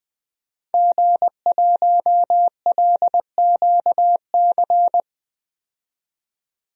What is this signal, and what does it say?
Morse code, heard as a keyed tone, about 20 wpm, 700 Hz. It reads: G1LQC
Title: DXCC-100 Call Signs - 5 Characters - 20wpm